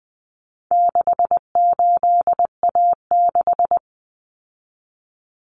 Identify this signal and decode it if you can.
Morse code, heard as a keyed tone, about 19 wpm, 700 Hz. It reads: 68A6
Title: Common Cut Numbers - 4-Digit 20wpm